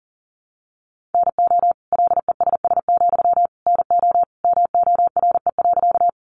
Morse code, heard as a keyed tone, about 40 wpm, 700 Hz. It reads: NO LESS, NO MORE.
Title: Sentences from Top 400 Words 40wpm